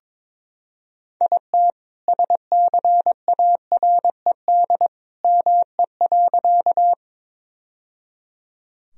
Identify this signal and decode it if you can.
Morse code, heard as a keyed tone, about 21 wpm, 700 Hz. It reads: IT SCARED ME.
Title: Sentences from Top 1000 Words - Review Part 4 - 22wpm